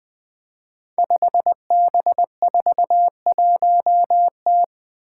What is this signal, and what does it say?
Morse code, heard as a keyed tone, about 20 wpm, 700 Hz. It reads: 5B41T
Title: Five Challenging Letter-Number Combinations 20wpm